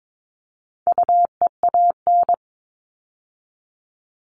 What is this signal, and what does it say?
Morse code, heard as a keyed tone, about 22 wpm, 700 Hz. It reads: UEAN
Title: All Cut Numbers - 4-Digit 22wpm